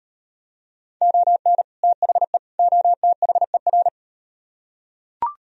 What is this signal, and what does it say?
Morse code, heard as a keyed tone, about 35 wpm, 700 Hz. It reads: ON THE OTHER